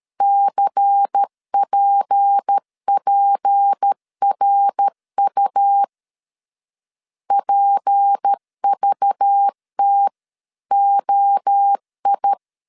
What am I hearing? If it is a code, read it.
Morse code, heard as a keyed tone, about 13 wpm, 800 Hz. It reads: CPPRU PVTOI
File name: LEZIONI / 6 PKX0 / 6 P 12wpm 2033